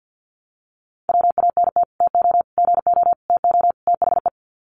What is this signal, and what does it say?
Morse code, heard as a keyed tone, about 52 wpm, 700 Hz. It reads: WANT TO GO TO THE
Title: Sets of 5 Words Encore2 50wpm